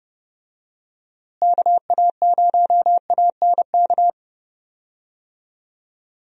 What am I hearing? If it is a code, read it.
Morse code, heard as a keyed tone, about 32 wpm, 700 Hz. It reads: KA0ANK
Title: USA Call Signs - 6 Characters - Repeated 3x - 30wpm